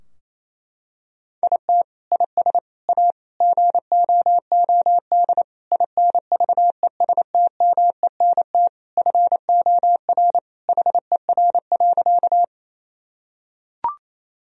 Morse code, heard as a keyed tone, about 28 wpm, 700 Hz. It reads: IT IS A GOOD INVESTMENT FOR HER.
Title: Sentences from Top 1000 Words - Review Part 4 - 28wpm